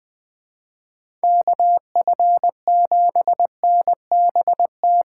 Morse code, heard as a keyed tone, about 19 wpm, 700 Hz. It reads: KF7NBT
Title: Speed Racing - USA 2x3 Call Signs 17wpm